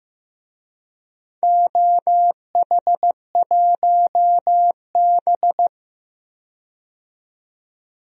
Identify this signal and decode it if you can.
Morse code, heard as a keyed tone, about 15 wpm, 700 Hz. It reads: OH1B
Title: DXCC-100 Call Signs - 4 Characters - 15wpm